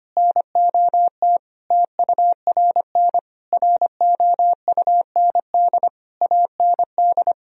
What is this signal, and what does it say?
Morse code, heard as a keyed tone, about 25 wpm, 700 Hz. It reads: NOT TURN ROUND AND
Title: Sentences from Top 1000 Words - Review Part 2 - 25wpm